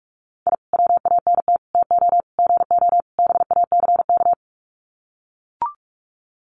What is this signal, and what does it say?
Morse code, heard as a keyed tone, about 44 wpm, 700 Hz. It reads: I WANT TO GO BACK
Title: Sets of 5 Words Encore 45wpm